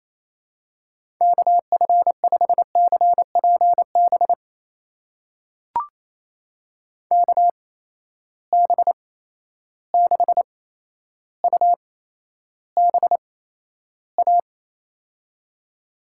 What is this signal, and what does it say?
Morse code, heard as a keyed tone, about 28 wpm, spaced as for 12 wpm, 700 Hz. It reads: KF5CPB KB6UBA
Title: USA Call Signs - 6 Characters - 8x Character Spacing - 28wpm